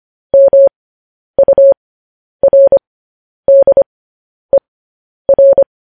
Morse code, heard as a keyed tone, about 25 wpm, spaced as for 10 wpm, 550 Hz. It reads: MURDER
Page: News Headlines in Morse Code at 10 WPM